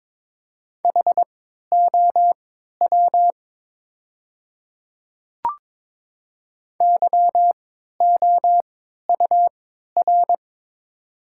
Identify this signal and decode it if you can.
Morse code, heard as a keyed tone, about 22 wpm, spaced as for 13 wpm, 700 Hz. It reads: HOW YOUR
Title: 3x Character Spacing - Top 100 Words - 22wpm